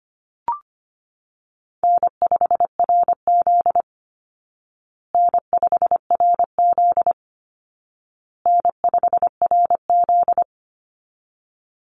Morse code, heard as a keyed tone, about 25 wpm, 700 Hz. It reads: N5RZ N5RZ N5RZ
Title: Call Signs Repeated 2x - 25wpm